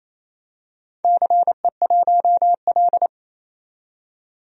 Morse code, heard as a keyed tone, about 28 wpm, 700 Hz. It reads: CE1L
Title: DXCC-100 Call Signs - 4 Characters - 28wpm